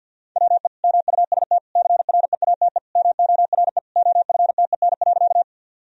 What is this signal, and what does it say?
Morse code, heard as a keyed tone, about 50 wpm, 700 Hz. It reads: WE MUST CREATE MORE OFTEN.
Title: Sentences from Top 1000 Words - Review Part 3 - 50wpm